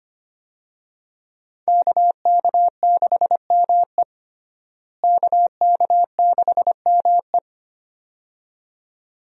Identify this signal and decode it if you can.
Morse code, heard as a keyed tone, about 25 wpm, 700 Hz. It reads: KK6ME KK6ME
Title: USA Call Signs - 5 Characters - Repeated 2x - 25wpm